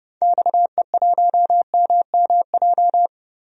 Morse code, heard as a keyed tone, about 32 wpm, 700 Hz. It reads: XE1MMJ
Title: DXCC-100 Call Signs - 6 Characters - Repeated 2x - 30wpm